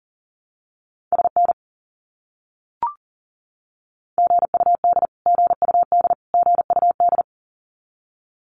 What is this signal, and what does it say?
Morse code, heard as a keyed tone, about 40 wpm, 700 Hz. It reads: SN GUD GUD GUD
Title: Abbreviations Repeated 3x - 40wpm